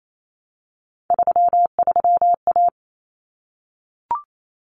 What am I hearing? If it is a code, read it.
Morse code, heard as a keyed tone, about 28 wpm, 700 Hz. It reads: 33A
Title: Common Cut Numbers - 3-Digit 28wpm